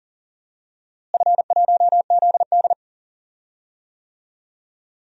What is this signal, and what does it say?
Morse code, heard as a keyed tone, about 37 wpm, 700 Hz. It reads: F1ZD
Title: DXCC-100 Call Signs - 4 Characters - Repeated 2x - 40wpm